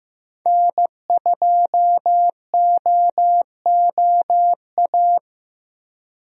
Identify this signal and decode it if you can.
Morse code, heard as a keyed tone, about 15 wpm, 700 Hz. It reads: N2OOA
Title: USA Call Signs - 5 Characters - Repeated 2x - 15wpm